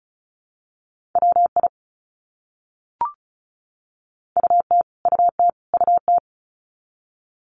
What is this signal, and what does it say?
Morse code, heard as a keyed tone, about 34 wpm, 700 Hz. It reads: WI UT UT UT
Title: US States Repeated 3x 35wpm